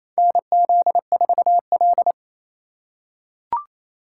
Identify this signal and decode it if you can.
Morse code, heard as a keyed tone, about 28 wpm, 700 Hz. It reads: NZ4L
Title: USA Call Signs - 2x Character Spacing - 28wpm